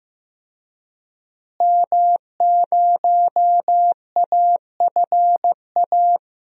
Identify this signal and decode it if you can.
Morse code, heard as a keyed tone, about 15 wpm, 700 Hz. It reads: M0AFA
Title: DXCC-100 Call Signs - 5 Characters - Repeated 2x - 15wpm